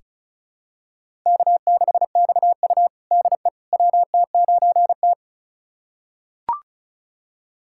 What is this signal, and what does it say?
Morse code, heard as a keyed tone, about 36 wpm, 700 Hz. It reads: K6XU DE WT9T
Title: USA Callsign DE Callsign - 4 Characters - 35wpm